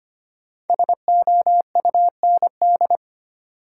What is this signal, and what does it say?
Morse code, heard as a keyed tone, about 25 wpm, 700 Hz. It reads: SOUND
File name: 5 Letter Words - First 3 Letters Given - 25wpm